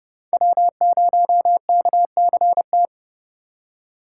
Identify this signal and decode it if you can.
Morse code, heard as a keyed tone, about 30 wpm, 700 Hz. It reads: W0KCT